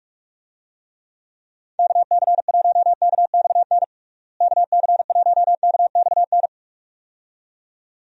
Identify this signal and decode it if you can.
Morse code, heard as a keyed tone, about 41 wpm, 700 Hz. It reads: KC1KXN KC1KXN
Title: USA Call Signs - 6 Characters - Repeated 2x - 45wpm